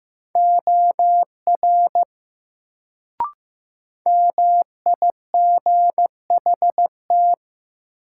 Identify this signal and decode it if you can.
Morse code, heard as a keyed tone, about 15 wpm, 700 Hz. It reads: OR MIGHT
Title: Sets of 2 Words - New Word Drill - 15wpm